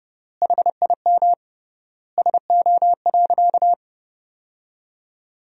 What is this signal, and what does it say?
Morse code, heard as a keyed tone, about 31 wpm, 700 Hz. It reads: HIM SO.
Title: 3x Word-Spacing - Sentences from Top 200 Words - 30wpm